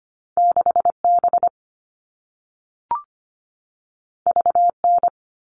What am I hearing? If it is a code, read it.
Morse code, heard as a keyed tone, about 25 wpm, 700 Hz. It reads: 6B VN
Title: All Cut Numbers - 2-Digit 25wpm